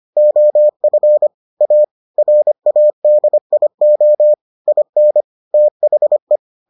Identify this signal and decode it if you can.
Morse code, heard as a keyed tone, about 25 wpm, 600 Hz. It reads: OF A RADIO IN THE